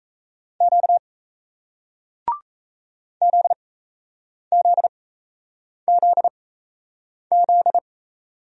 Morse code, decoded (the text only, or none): Q ZZZZ